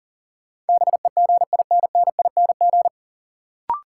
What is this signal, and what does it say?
Morse code, heard as a keyed tone, about 40 wpm, 700 Hz. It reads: BEGINNING
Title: Sentences from Top 800 Words - New Word Drill - 40wpm